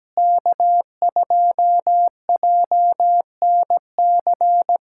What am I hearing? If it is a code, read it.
Morse code, heard as a keyed tone, about 17 wpm, 700 Hz. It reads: K2JNC
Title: USA Callsign DE Callsign - 5 Characters - 17wpm